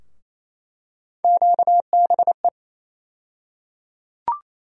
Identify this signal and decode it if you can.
Morse code, heard as a keyed tone, about 27 wpm, 700 Hz. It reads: QBE
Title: USA Call Sign Postfixes - 3 Letters 28wpm